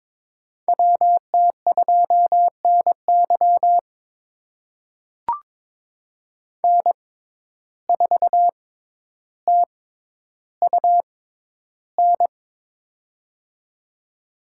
Morse code, heard as a keyed tone, about 22 wpm, spaced as for 11 wpm, 700 Hz. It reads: WT2NY N4TUN
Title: USA Call Signs - 5 Characters - 6x Character Spacing - 22wpm